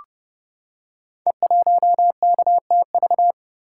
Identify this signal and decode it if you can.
Morse code, heard as a keyed tone, about 32 wpm, 700 Hz. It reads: E1KTV
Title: Code Groups - 5 Letter:Numbers - 30wpm